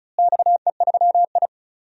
Morse code, heard as a keyed tone, about 36 wpm, 700 Hz. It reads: XE3I
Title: DXCC-100 Call Signs - 4 Characters - Repeated 2x - 35wpm